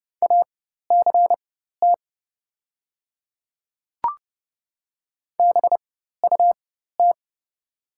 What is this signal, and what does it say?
Morse code, heard as a keyed tone, about 30 wpm, spaced as for 14 wpm, 700 Hz. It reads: ACT BUT
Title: 4x Character Spacing - Top 200 Words - 30wpm